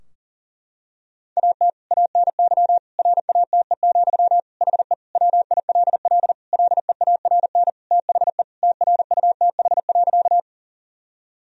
Morse code, heard as a keyed tone, about 40 wpm, 700 Hz. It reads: AT ANY RATE, HE WILL LEARN THE TRUTH.
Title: Sentences from Top 1000 Words - Review Part 4 - 40wpm